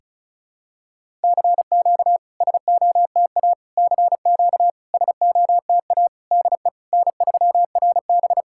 Morse code, heard as a keyed tone, about 36 wpm, 700 Hz. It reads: CQ SOTA CQ SOTA DE N3RB
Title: SOTA QSO Practice 35wpm